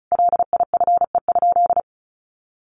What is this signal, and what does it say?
Morse code, heard as a keyed tone, about 34 wpm, 700 Hz. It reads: LIFE?
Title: Sentences from Top 1500 Words 35wpm